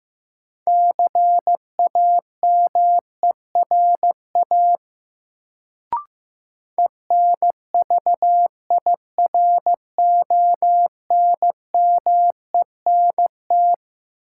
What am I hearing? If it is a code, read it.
Morse code, heard as a keyed tone, about 15 wpm, 700 Hz. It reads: CAMERA ENVIRONMENT
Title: Sentences from Top 1000 Words - Review Part 5 - New Word Drill - 15wpm